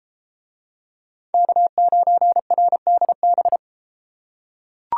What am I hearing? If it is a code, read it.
Morse code, heard as a keyed tone, about 32 wpm, 700 Hz. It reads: K9RDB